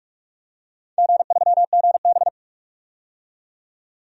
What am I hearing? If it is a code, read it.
Morse code, heard as a keyed tone, about 45 wpm, 700 Hz. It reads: G3GB
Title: DXCC-100 Call Signs - 4 Characters - 45wpm